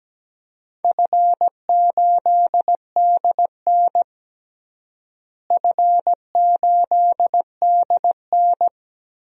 Morse code, heard as a keyed tone, about 17 wpm, 700 Hz. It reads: F8DN F8DN
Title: DXCC-100 Call Signs - 4 Characters - Repeated 2x - 17wpm